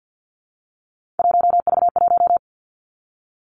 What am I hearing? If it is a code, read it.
Morse code, heard as a keyed tone, about 50 wpm, 700 Hz. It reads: 1V1
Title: Three Challenging Letter-Number Combinations 50wpm